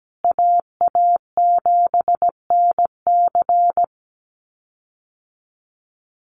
Morse code, heard as a keyed tone, about 17 wpm, 700 Hz. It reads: AA7NC
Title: USA Call Signs Repeated 3x 17wpm